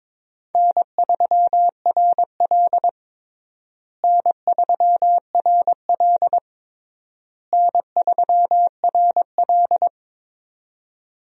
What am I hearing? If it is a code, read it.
Morse code, heard as a keyed tone, about 21 wpm, 700 Hz. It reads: N3RL N3RL N3RL
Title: USA Call Signs - 4 Characters - Repeated 3x - 22wpm